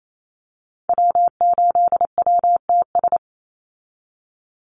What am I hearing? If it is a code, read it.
Morse code, heard as a keyed tone, about 28 wpm, 700 Hz. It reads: W8WTS
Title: Call Signs Repeated 2x - 28wpm